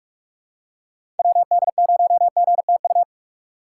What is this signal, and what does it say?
Morse code, heard as a keyed tone, about 47 wpm, 700 Hz. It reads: WD0GTU
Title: USA Call Signs - 6 Characters - 6x Character Spacing - 45wpm